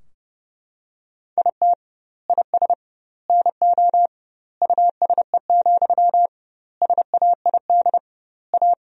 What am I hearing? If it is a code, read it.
Morse code, heard as a keyed tone, about 30 wpm, 700 Hz. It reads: IT IS NO USE, SAID A
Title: 2x Word-Spacing - Sentences from Top 500 Words - 30wpm